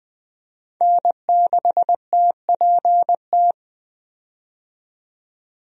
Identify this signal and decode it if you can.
Morse code, heard as a keyed tone, about 20 wpm, 700 Hz. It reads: N6TPT
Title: USA Call Signs - 5 Characters - 20wpm